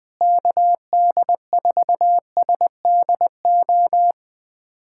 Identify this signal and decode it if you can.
Morse code, heard as a keyed tone, about 21 wpm, 700 Hz. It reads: KD4SDO
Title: USA Callsign DE Callsign 20wpm